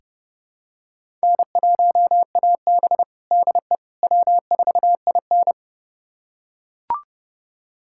Morse code, heard as a keyed tone, about 32 wpm, 700 Hz. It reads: N1AB DE W4IN
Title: USA Callsign DE Callsign - 4 Characters - 30wpm